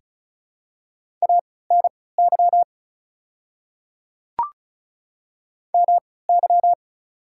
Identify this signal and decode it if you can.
Morse code, heard as a keyed tone, about 36 wpm, spaced as for 20 wpm, 700 Hz. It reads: ANY MY
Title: 3x Character Spacing - Top 200 Words - 35wpm